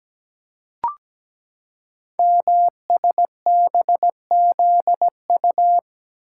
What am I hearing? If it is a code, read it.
Morse code, heard as a keyed tone, about 17 wpm, 700 Hz. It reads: MSBZU
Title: Code Groups - 5 Letters - 17wpm